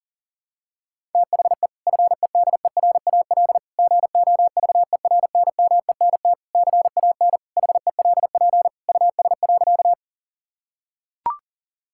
Morse code, heard as a keyed tone, about 40 wpm, 700 Hz. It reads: THE FEDERAL GOVERNMENT CAN HELP US.
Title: Sentences from Top 1000 Words - Review Part 5 - 40wpm